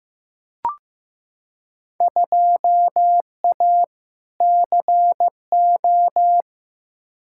Morse code, heard as a keyed tone, about 15 wpm, 700 Hz. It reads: 2A CO
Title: Field Day Exchanges - 15wpm